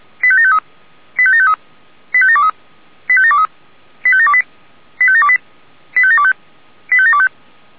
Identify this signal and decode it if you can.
Five-tone selective call (ZVEI), heard as quick runs of five tones, tones of about 70 ms. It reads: 85753 85753 85723 85723 85728 85728 85725 85725